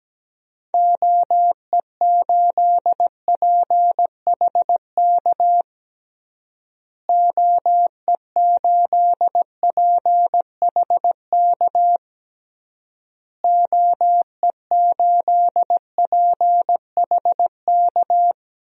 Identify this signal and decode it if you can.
Morse code, heard as a keyed tone, about 17 wpm, 700 Hz. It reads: OE8PHK OE8PHK OE8PHK
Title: DXCC-100 Call Signs - 6 Characters - Repeated 3x - 17wpm